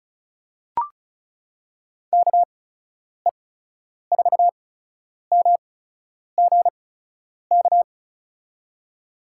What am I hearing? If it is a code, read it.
Morse code, heard as a keyed tone, about 36 wpm, spaced as for 10 wpm, 700 Hz. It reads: KE4MGK
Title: USA Call Signs - 6 Characters - 8x Character Spacing - 35wpm